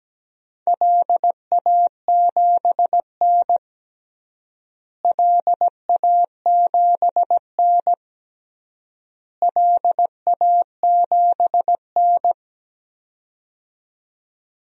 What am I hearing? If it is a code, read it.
Morse code, heard as a keyed tone, about 17 wpm, 700 Hz. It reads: LA7N LA7N LA7N
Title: DXCC-100 Call Signs - 4 Characters - Repeated 3x - 17wpm